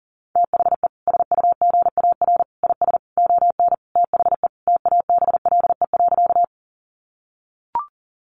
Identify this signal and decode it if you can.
Morse code, heard as a keyed tone, about 40 wpm, 700 Hz. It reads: THE SUGAR IS ON THE TABLE.
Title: Sentences from Top 1200 Words 40wpm